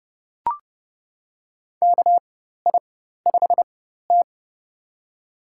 Morse code, heard as a keyed tone, about 32 wpm, spaced as for 14 wpm, 700 Hz. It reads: KI5T
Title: USA Call Signs - 4x Character Spacing - 30wpm